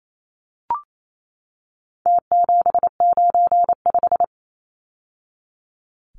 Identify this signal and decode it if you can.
Morse code, heard as a keyed tone, about 28 wpm, 700 Hz. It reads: T795
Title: Numbers - Contest Style - 4-Digit 28wpm